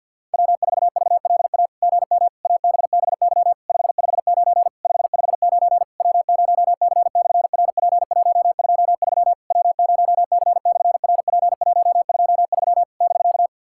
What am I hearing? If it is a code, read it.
Morse code, heard as a keyed tone, about 50 wpm, 700 Hz. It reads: W4VLA GM ABBY 559 559 W0C/RP123 W0C/RP123 <BK>
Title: SOTA QSO Practice 50wpm